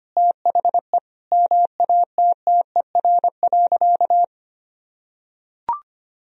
Morse code, heard as a keyed tone, about 25 wpm, 700 Hz. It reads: THE MATTER.
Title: Sentences from Top 1000 Words - Review Part 3 - 25wpm